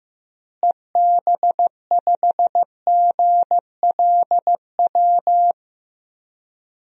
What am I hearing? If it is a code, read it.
Morse code, heard as a keyed tone, about 15 wpm, 700 Hz. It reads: EB5GLW
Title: DXCC-100 Call Signs - 6 Characters - Repeated 3x - 15wpm